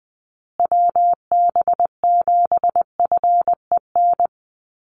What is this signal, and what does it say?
Morse code, heard as a keyed tone, about 20 wpm, 700 Hz. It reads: WB7FEN
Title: USA Call Signs Repeated 3x 20wpm